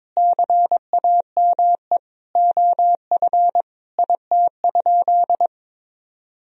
Morse code, heard as a keyed tone, about 22 wpm, 700 Hz. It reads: CAME OF IT?
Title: Sentences from Top 1000 Words - Review Part 3 - 22wpm